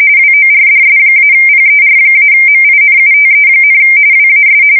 RTTY, about 45.45 baud. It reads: WORDS-PER-MINUTE, WAS
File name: fsk170 / teletype-corp